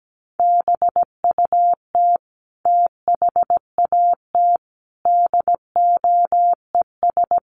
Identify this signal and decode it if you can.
Morse code, heard as a keyed tone, about 17 wpm, 700 Hz. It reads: BUT THAT DOES
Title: Sets of 5 Words 17wpm